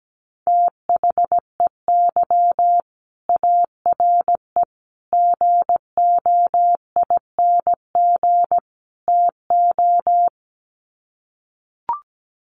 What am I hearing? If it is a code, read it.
Morse code, heard as a keyed tone, about 17 wpm, 700 Hz. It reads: THEY ARE GOING TO
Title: Sets of 4 Words 17wpm